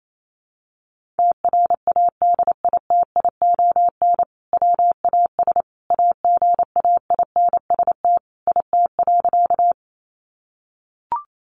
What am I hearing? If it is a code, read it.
Morse code, heard as a keyed tone, about 28 wpm, 700 Hz. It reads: TRADITION WAS AGAINST IT.
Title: Sentences from Top 1500 Words 28wpm